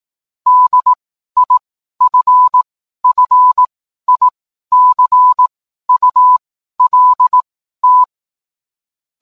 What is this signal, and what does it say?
Morse code, heard as a keyed tone, about 18 wpm, spaced as for 12 wpm, 1000 Hz. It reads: DIFFICULT